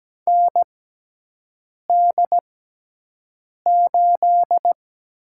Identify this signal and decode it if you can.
Morse code, heard as a keyed tone, about 17 wpm, spaced as for 6 wpm, 700 Hz. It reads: ND8
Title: USA Call Signs - 6x Character Spacing - 17wpm